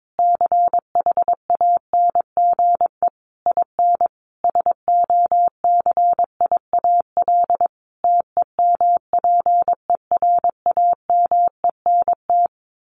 Morse code, heard as a keyed tone, about 22 wpm, 700 Hz. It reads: CHANGE IN SOCIAL TEMPERAMENT
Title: A Sound Of Thunder 22wpm